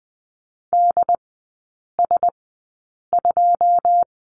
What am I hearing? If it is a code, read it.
Morse code, heard as a keyed tone, about 20 wpm, spaced as for 8 wpm, 700 Hz. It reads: DS2
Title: 081 Lesson 14e - Number 2 - Three Characters 20wpm